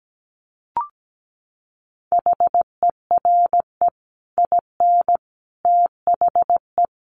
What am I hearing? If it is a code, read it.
Morse code, heard as a keyed tone, about 17 wpm, 700 Hz. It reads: HERE IN THE
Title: Sets of 5 Words 17wpm